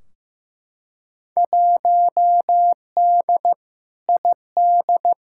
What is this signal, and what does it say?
Morse code, heard as a keyed tone, about 15 wpm, 700 Hz. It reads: 1D ID
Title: Field Day Exchanges - 15wpm